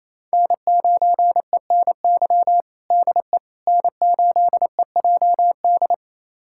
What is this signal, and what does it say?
Morse code, heard as a keyed tone, about 28 wpm, 700 Hz. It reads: N9ENY DE N8EJD
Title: USA Callsign DE Callsign - 5 Characters - 28wpm